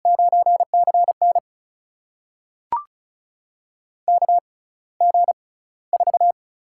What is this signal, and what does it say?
Morse code, heard as a keyed tone, about 36 wpm, spaced as for 18 wpm, 700 Hz. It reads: 9CN KG4
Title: USA Call Signs - 4 Characters - 6x Character Spacing - 35wpm